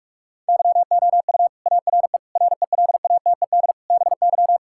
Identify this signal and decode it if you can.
Morse code, heard as a keyed tone, about 46 wpm, 700 Hz. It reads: YOU ARE RELATED BY
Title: Sentences from Top 1000 Words - Review Part 5 - 45wpm